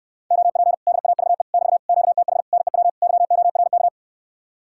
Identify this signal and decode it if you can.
Morse code, heard as a keyed tone, about 69 wpm, 700 Hz. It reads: OP DAVE <BT> QTH NEW YORK